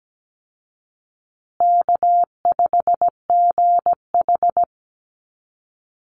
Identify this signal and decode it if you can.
Morse code, heard as a keyed tone, about 17 wpm, 700 Hz. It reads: K5GH
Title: Call Signs Repeated 2x - 17wpm